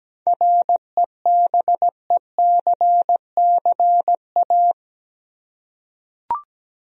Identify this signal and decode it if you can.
Morse code, heard as a keyed tone, about 17 wpm, 700 Hz. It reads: REBECCA
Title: USA Common Names - 17wpm